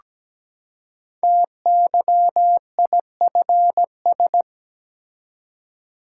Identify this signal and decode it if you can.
Morse code, heard as a keyed tone, about 17 wpm, 700 Hz. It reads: TYIFS
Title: Code Groups - 5 Letter:Numbers - 17wpm